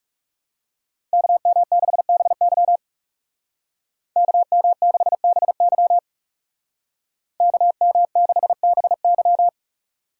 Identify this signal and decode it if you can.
Morse code, heard as a keyed tone, about 36 wpm, 700 Hz. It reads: KM6BY KM6BY KM6BY
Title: Speed Racing - USA 2x2 Call Signs 30wpm